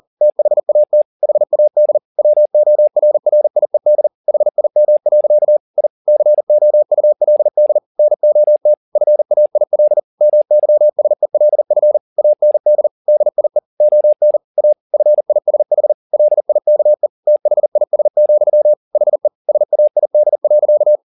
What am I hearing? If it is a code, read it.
Morse code, heard as a keyed tone, about 40 wpm, 600 Hz. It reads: THAT HAD WORRIED HIM. I COULD NOT FAIL MYSELF AND DIE ON A FISH LIKE THIS, HE SAID.
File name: the old man and the sea 2 40wpm